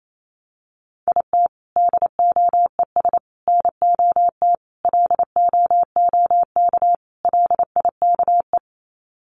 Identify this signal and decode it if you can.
Morse code, heard as a keyed tone, about 28 wpm, 700 Hz. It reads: IT DOES NOT LOOK LIKE
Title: Sets of 5 Words Encore2 28wpm